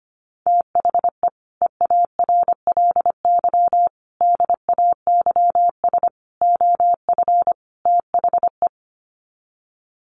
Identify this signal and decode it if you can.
Morse code, heard as a keyed tone, about 25 wpm, 700 Hz. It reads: THE EARLY DAYS OF THE
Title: Sets of 5 Words Encore 25wpm